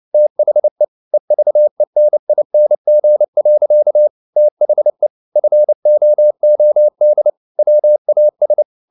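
Morse code, heard as a keyed tone, about 29 wpm, 600 Hz. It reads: THE EVENING. THE FOOD WAS
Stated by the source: Morse code at 29 wpm, sending Ken Follett's novel Never in English, from CW med Gnister.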